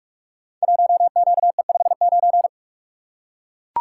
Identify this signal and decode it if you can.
Morse code, heard as a keyed tone, about 41 wpm, 700 Hz. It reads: 1QE59
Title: Code Groups - 5 Letter:Numbers - 45wpm